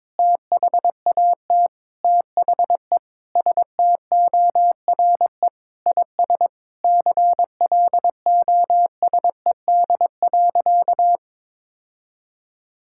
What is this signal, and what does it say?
Morse code, heard as a keyed tone, about 22 wpm, 700 Hz. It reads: THAT THE STORE IS CLOSED.
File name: Sentences with 7-Word Phrases - 22wpm